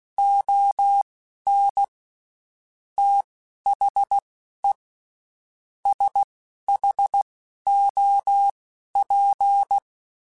Morse code, heard as a keyed tone, about 16 wpm, spaced as for 11 wpm, 800 Hz. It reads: ON THE SHOP
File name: Spaced Characters / Spaced Characters-mp3s / Weather4 24 sp @15